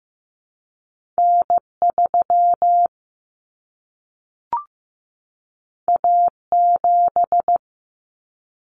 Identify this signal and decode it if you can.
Morse code, heard as a keyed tone, about 15 wpm, 700 Hz. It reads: N3 A7
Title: Common Cut Numbers - 2-Digit 15wpm